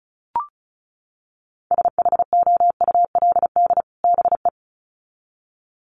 Morse code, heard as a keyed tone, about 34 wpm, 700 Hz. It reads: SHOULD BE
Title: Sets of 2 Words 35wpm